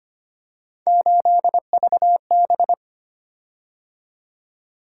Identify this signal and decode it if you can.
Morse code, heard as a keyed tone, about 25 wpm, 700 Hz. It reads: 8VB
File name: Three Letter-Number Combinations - 25wpm